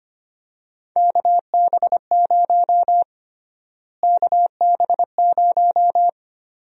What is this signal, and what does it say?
Morse code, heard as a keyed tone, about 25 wpm, 700 Hz. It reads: KB0 KB0
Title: USA Call Sign Prefixes - Repeated 2x - 25wpm